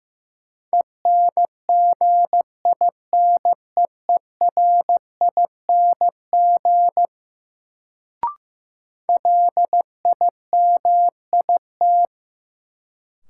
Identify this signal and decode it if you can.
Morse code, heard as a keyed tone, about 15 wpm, 700 Hz. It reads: ENGINEERING LIMIT
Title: Sentences from Top 1500 Words - New Word Drill Alt - 15wpm